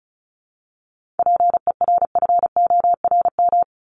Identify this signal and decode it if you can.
Morse code, heard as a keyed tone, about 36 wpm, 700 Hz. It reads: PERFORM
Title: Top 751-1000 Words 35wpm